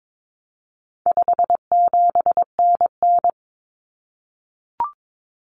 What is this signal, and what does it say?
Morse code, heard as a keyed tone, about 22 wpm, 700 Hz. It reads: 57NN
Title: Common Cut Numbers - 4-Digit 22wpm